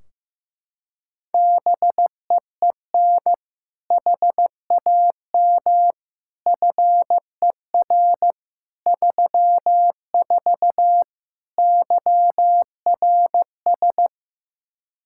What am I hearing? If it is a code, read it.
Morse code, heard as a keyed tone, about 15 wpm, 700 Hz. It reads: BEEN HAM FER 34 YRS
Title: Practice QSO - Exchange 3 - More Info - 15wpm